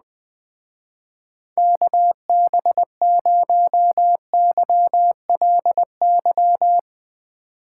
Morse code, heard as a keyed tone, about 20 wpm, 700 Hz. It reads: KB0YLY